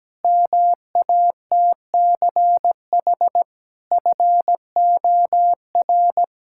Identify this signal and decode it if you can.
Morse code, heard as a keyed tone, about 17 wpm, 700 Hz. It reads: MATCH FOR
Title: Sentences from Top 1600 Words 17wpm